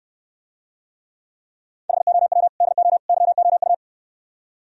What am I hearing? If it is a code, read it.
Morse code, heard as a keyed tone, about 69 wpm, 700 Hz. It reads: HOW DO YOU